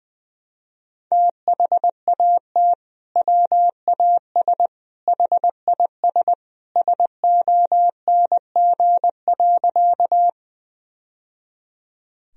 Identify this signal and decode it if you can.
Morse code, heard as a keyed tone, about 20 wpm, 700 Hz. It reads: THAT WAS HIS SONG.
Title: Sentences from Top 1000 Words - Review Part 2 - 20wpm